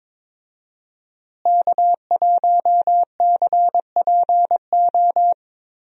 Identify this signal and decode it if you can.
Morse code, heard as a keyed tone, about 22 wpm, 700 Hz. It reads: K1CPO
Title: USA Call Signs - 5 Characters - Repeated 2x - 22wpm